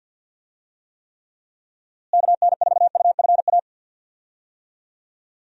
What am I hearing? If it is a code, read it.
Morse code, heard as a keyed tone, about 52 wpm, 700 Hz. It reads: KN4UFA
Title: USA Call Signs - 6 Characters - Repeated 3x - 50wpm